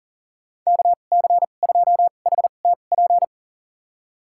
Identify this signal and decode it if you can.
Morse code, heard as a keyed tone, about 40 wpm, spaced as for 29 wpm, 700 Hz. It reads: KC2HTP
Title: USA Call Signs - 6 Characters - 2x Character Spacing - 40wpm